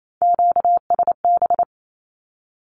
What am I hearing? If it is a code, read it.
Morse code, heard as a keyed tone, about 28 wpm, 700 Hz. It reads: QSB
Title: Q-Codes 28wpm